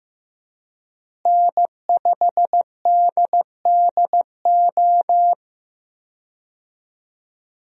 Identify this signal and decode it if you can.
Morse code, heard as a keyed tone, about 15 wpm, 700 Hz. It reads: N5DDO
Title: USA Call Signs - 5 Characters - Repeated 3x - 15wpm